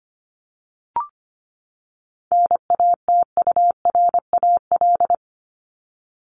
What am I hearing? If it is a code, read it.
Morse code, heard as a keyed tone, about 25 wpm, 700 Hz. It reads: NATURAL
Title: Top 501-750 Words 25wpm